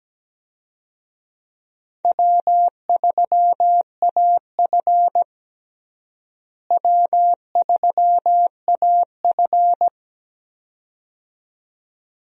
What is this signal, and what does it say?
Morse code, heard as a keyed tone, about 17 wpm, 700 Hz. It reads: W3AF W3AF
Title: USA Call Signs - 4 Characters - Repeated 2x - 17wpm